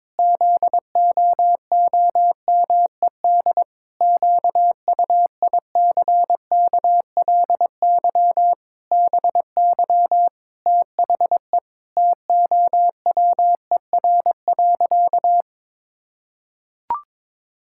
Morse code, heard as a keyed tone, about 22 wpm, 700 Hz. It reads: ZOOMED QUICKLY BY THE TOWER.
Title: Pangram Challenge v2 - 22wpm